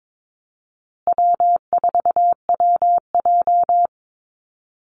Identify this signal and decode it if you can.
Morse code, heard as a keyed tone, about 22 wpm, 700 Hz. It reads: W4WJ
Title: Easy Call Signs Repeated 3x - 22wpm